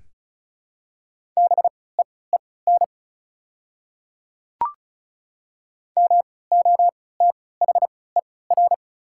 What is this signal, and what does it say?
Morse code, heard as a keyed tone, about 36 wpm, spaced as for 20 wpm, 700 Hz. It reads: BEEN MOTHER
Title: 3x Character Spacing - Top 200 Words - 35wpm